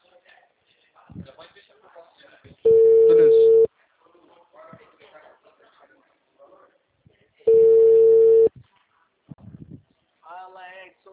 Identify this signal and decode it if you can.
Ringback tone (China, 450 Hz), with speech.